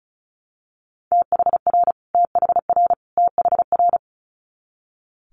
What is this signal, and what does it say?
Morse code, heard as a keyed tone, about 36 wpm, 700 Hz. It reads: THR THR THR
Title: Multi-letter Phonemes Repeated 3x 35wpm